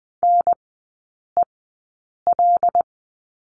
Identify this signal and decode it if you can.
Morse code, heard as a keyed tone, about 20 wpm, spaced as for 8 wpm, 700 Hz. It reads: NEL
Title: 159 Lesson 26e - Letter g - Three Characters 20wpm